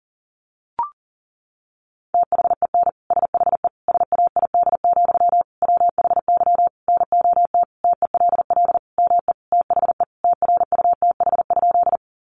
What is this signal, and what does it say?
Morse code, heard as a keyed tone, about 40 wpm, 700 Hz. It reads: THEN SHE SAID, WHY NOT TELL ME THE TRUTH?
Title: Sentences from Top 800 Words 40wpm